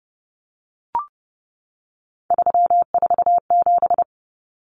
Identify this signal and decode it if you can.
Morse code, heard as a keyed tone, about 30 wpm, 700 Hz. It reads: 347
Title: Numbers - 3-Digit 30wpm